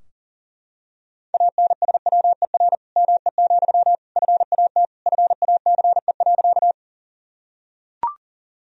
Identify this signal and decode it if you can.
Morse code, heard as a keyed tone, about 40 wpm, 700 Hz. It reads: ANSWER ME, FAT FACE.
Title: Sentences from Top 1000 Words - Review Part 4 - 40wpm